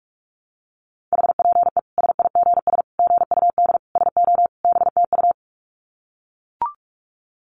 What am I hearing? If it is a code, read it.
Morse code, heard as a keyed tone, about 44 wpm, 700 Hz. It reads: HPE SIGS GUD SO BTU
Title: Common QSO Phrases 45wpm